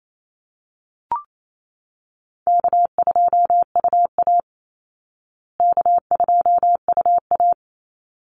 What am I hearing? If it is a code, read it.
Morse code, heard as a keyed tone, about 28 wpm, 700 Hz. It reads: K2UA K2UA
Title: Easy Call Signs Repeated 2x - 28wpm